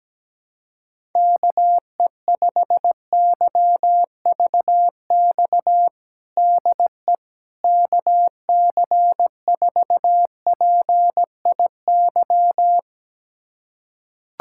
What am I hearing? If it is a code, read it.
Morse code, heard as a keyed tone, about 17 wpm, 700 Hz. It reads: KE5YVX DE KC4PIY
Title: USA Callsign DE Callsign - 6 Characters - 17wpm